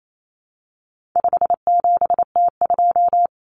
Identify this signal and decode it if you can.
Morse code, heard as a keyed tone, about 28 wpm, 700 Hz. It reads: 57T2
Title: Common Cut Numbers - 4-Digit 28wpm